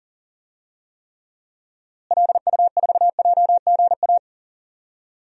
Morse code, heard as a keyed tone, about 40 wpm, 700 Hz. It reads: LU4JGA
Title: DXCC-100 Call Signs - 6 Characters - Repeated 3x - 40wpm